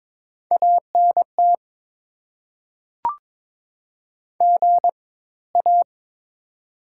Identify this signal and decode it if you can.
Morse code, heard as a keyed tone, about 22 wpm, spaced as for 16 wpm, 690 Hz. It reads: ANT GA
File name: Top 20 QSO Elements - 4x Character Spacing - 22wpm